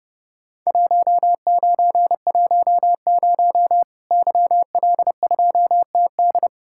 Morse code, heard as a keyed tone, about 30 wpm, 700 Hz. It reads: 1910 YL2TD
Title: Worked All Europe DX Contest - QTC Exchange - 30wpm